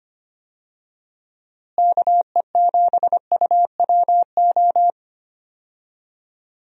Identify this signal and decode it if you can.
Morse code, heard as a keyed tone, about 25 wpm, 700 Hz. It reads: KE7UWO
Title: USA Call Signs - 6 Characters - Repeated 3x - 25wpm